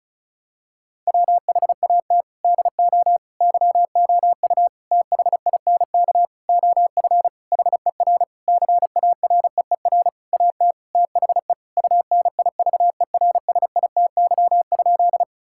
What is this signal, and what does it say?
Morse code, heard as a keyed tone, about 36 wpm, 700 Hz. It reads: WHAT DO YOU THINK OF HER CAREER AT THE UNIVERSITY?
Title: Sentences from Top 1000 Words - Review Part 6 - 35wpm